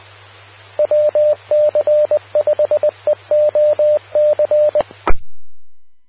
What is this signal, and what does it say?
Morse code, heard as a keyed tone, about 20 wpm, 600 Hz, with radio noise. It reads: WC5EOC
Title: Austin, TX Amateur Repeaters